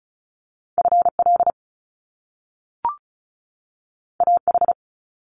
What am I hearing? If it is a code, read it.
Morse code, heard as a keyed tone, about 36 wpm, 700 Hz. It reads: FL AH
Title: Top Two Letter Combinations in Words 35wpm